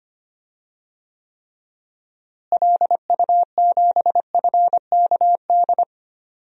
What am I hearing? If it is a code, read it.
Morse code, heard as a keyed tone, about 25 wpm, 700 Hz. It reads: LU7FKD
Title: DXCC-100 Call Signs - 6 Characters - Repeated 2x - 25wpm